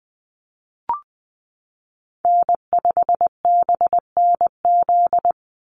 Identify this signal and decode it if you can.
Morse code, heard as a keyed tone, about 20 wpm, 700 Hz. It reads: N5BNZ
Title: USA Call Signs Repeated 3x 20wpm